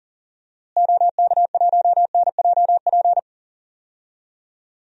Morse code, heard as a keyed tone, about 39 wpm, 700 Hz. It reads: OK1NJP
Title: DXCC-100 Call Signs - 6 Characters - Repeated 3x - 40wpm